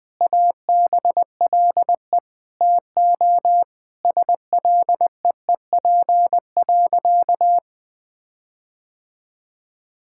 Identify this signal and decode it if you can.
Morse code, heard as a keyed tone, about 20 wpm, 700 Hz. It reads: ABLE TO SLEEP.
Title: Sentences with 5-Word Phrases - Spoken Beginning - 20wpm